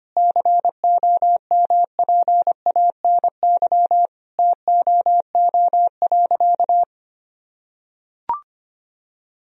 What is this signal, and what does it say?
Morse code, heard as a keyed tone, about 25 wpm, 700 Hz. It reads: COMPANY TOO.
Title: Sentences from Top 1000 Words - Review Part 3 - 25wpm